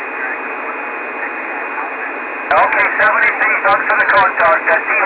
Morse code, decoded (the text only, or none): none